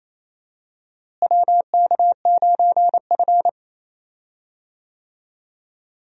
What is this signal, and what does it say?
Morse code, heard as a keyed tone, about 27 wpm, 700 Hz. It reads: WK9F
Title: USA Call Signs - 4 Characters - Repeated 3x - 28wpm